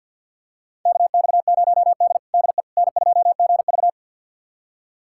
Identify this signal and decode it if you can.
Morse code, heard as a keyed tone, about 51 wpm, 700 Hz. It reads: KX0D DE N2GV
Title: USA Callsign DE Callsign - 4 Characters - 50wpm